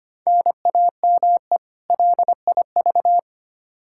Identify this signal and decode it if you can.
Morse code, heard as a keyed tone, about 25 wpm, 700 Hz. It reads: NAME LIV
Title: Practice QSO - Exchange 1 - Beginning - 25wpm